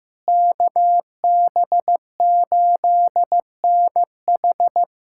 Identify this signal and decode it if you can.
Morse code, heard as a keyed tone, about 15 wpm, 700 Hz. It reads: KB8NH
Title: USA Callsign DE Callsign - 5 Characters - 15wpm